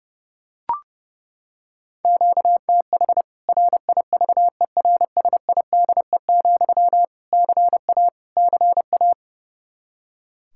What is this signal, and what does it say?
Morse code, heard as a keyed tone, about 30 wpm, 700 Hz. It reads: QTH RIVERSIDE, CA CA
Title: K1USN SST QSO Practice 30wpm